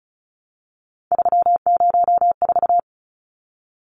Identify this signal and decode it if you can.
Morse code, heard as a keyed tone, about 34 wpm, 700 Hz. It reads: 304
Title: Numbers - 3-Digit 35wpm